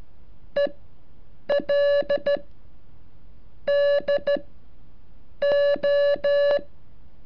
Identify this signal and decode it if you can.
Morse code, heard as a keyed tone, about 14 wpm, spaced as for 7 wpm, 570 Hz. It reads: EL DO
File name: curso / ejercicio 36